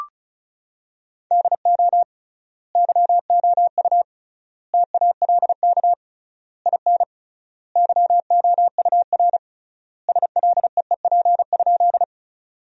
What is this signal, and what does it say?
Morse code, heard as a keyed tone, about 34 wpm, 700 Hz. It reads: DO YOU TALK IN YOUR SLEEP?
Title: 3x Word-Spacing - Sentences from Top 500 Words - 35wpm